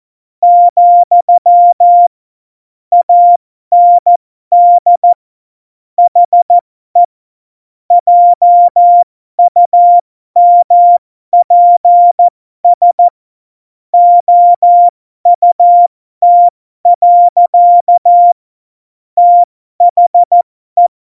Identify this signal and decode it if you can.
Morse code, heard as a keyed tone, about 14 wpm, spaced as for 12 wpm, 700 Hz. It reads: , AND HE JUMPS OUT. THE